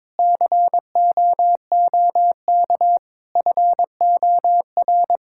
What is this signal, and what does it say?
Morse code, heard as a keyed tone, about 22 wpm, 700 Hz. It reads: COOK FOR
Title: Sentences from Top 1600 Words 22wpm